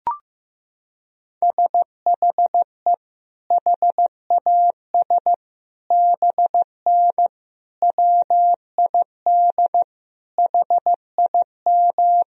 Morse code, heard as a keyed tone, about 15 wpm, 700 Hz. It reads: SHE HAS BN WID HIM
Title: Sentences from Top 200 Words and Abbreviations 15wpm